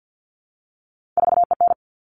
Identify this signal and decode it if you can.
Morse code, heard as a keyed tone, about 50 wpm, 700 Hz. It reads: 4EN
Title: All Cut Numbers - 3-Digit 50wpm